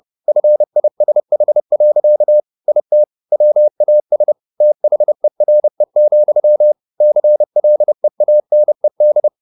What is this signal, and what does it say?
Morse code, heard as a keyed tone, about 30 wpm, 600 Hz. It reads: FISH. IT WAS THERE, CLEANED